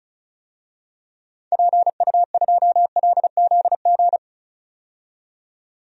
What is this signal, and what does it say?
Morse code, heard as a keyed tone, about 36 wpm, 700 Hz. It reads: PU2LZG
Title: DXCC-100 Call Signs - 6 Characters - 35wpm